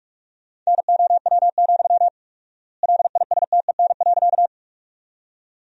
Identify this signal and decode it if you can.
Morse code, heard as a keyed tone, about 45 wpm, 700 Hz. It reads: NOW, LISTEN.
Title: 4x Word-Spacing - Sentences from Top 400 Words - 45wpm